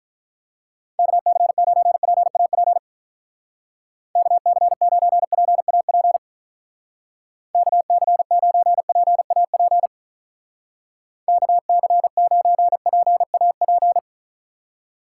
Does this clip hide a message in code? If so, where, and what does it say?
Yes: Morse code from 1.0 to 14.0 s, heard as a keyed tone, about 44 wpm, 700 Hz. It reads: KC9PAP KC9PAP KC9PAP KC9PAP